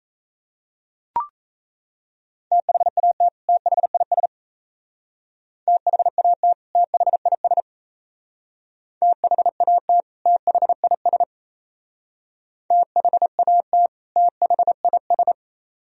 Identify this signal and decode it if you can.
Morse code, heard as a keyed tone, about 33 wpm, 700 Hz. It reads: THAT THIS THAT THIS THAT THIS THAT THIS
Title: Speed-Racing - Sets of 2 Words 28wpm